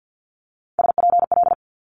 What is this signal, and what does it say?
Morse code, heard as a keyed tone, about 50 wpm, 700 Hz. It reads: SPL
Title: Multi-letter Phonemes Quick Fire 50wpm